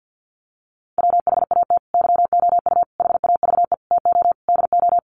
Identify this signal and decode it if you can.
Morse code, heard as a keyed tone, about 50 wpm, 700 Hz. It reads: WHAT YOU HAVE TO DO
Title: Sets of 5 Words Encore 50wpm